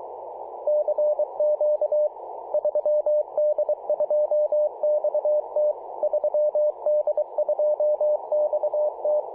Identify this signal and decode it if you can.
Morse code, heard as a keyed tone, about 23 wpm, 600 Hz. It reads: CQ 3D2XT 3D2XT
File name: posts / apr-13-2119z-3d2xt-21-021mhz-cw